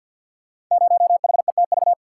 Morse code, heard as a keyed tone, about 52 wpm, 700 Hz. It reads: 0HETV
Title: Code Groups - 5 Letter:Numbers - 50wpm